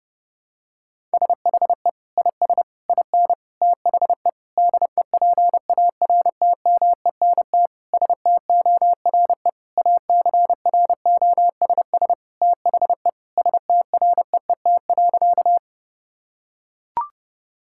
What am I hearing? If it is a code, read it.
Morse code, heard as a keyed tone, about 30 wpm, 700 Hz. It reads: SHE IS IN THE DEPARTMENT STORE ACROSS THE STREET.
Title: Sentences from Top 1000 Words - Review Part 5 - 30wpm